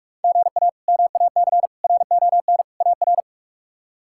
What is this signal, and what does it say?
Morse code, heard as a keyed tone, about 46 wpm, 700 Hz. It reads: GA MAC RON AR
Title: K1USN SST QSO Practice - 45wpm